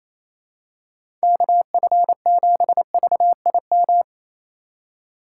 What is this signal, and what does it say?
Morse code, heard as a keyed tone, about 28 wpm, 700 Hz. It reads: KF7VIM
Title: USA Call Signs - 6 Characters - 3x Character Spacing - 28wpm